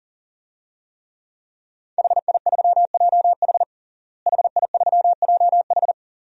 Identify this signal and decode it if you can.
Morse code, heard as a keyed tone, about 40 wpm, 690 Hz. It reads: HI3JH HI3JH
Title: DXCC-100 Call Signs - 5 Characters - Repeated 2x - 40wpm